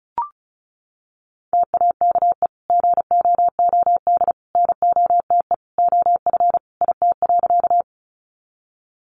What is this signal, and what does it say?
Morse code, heard as a keyed tone, about 34 wpm, 700 Hz. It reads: TAKE GOOD NOTE OF IT.
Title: Sentences from Top 500 Words 35wpm